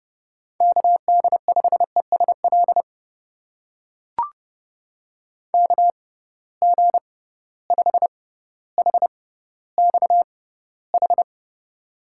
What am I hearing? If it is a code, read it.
Morse code, heard as a keyed tone, about 32 wpm, spaced as for 16 wpm, 700 Hz. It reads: KD5ESL KG5HXH
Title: USA Call Signs - 6x Character Spacing - 30wpm